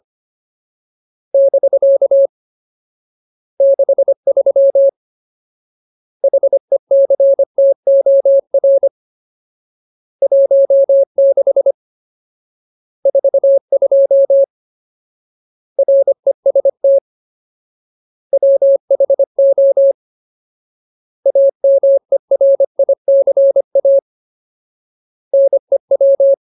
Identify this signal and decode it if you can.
Morse code, heard as a keyed tone, about 25 wpm, 550 Hz. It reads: <BK> 63 HECTOR 16 42 REST WHO AMERICA NEW
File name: Ultimate Word List / 25WPM Ultimate